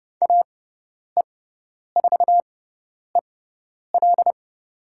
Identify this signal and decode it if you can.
Morse code, heard as a keyed tone, about 30 wpm, spaced as for 10 wpm, 700 Hz. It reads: AE4EL
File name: Lesson 16g - Letter c - Call Signs 30wpm